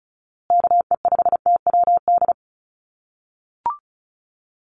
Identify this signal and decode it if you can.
Morse code, heard as a keyed tone, about 34 wpm, 700 Hz. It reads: KE5TWD
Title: USA Call Signs Repeated 3x 35wpm